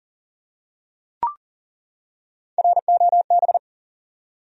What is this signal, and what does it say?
Morse code, heard as a keyed tone, about 40 wpm, 700 Hz. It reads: ROB
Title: USA Common Names - Max 5 Letters - 40wpm